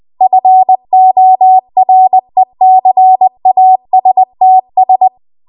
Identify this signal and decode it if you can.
Morse code, heard as a keyed tone, about 20 wpm, 750 Hz. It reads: FORECASTS
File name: Archive / 20 WPM / 20WPM